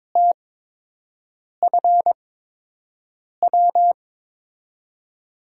Morse code, heard as a keyed tone, about 22 wpm, spaced as for 6 wpm, 700 Hz. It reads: TFW
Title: USA Call Signs - 5 Characters - 8x Character Spacing - 22wpm